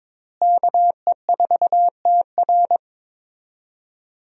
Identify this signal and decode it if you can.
Morse code, heard as a keyed tone, about 22 wpm, 700 Hz. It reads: KE4TR
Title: USA Call Signs - 5 Characters - Repeated 3x - 22wpm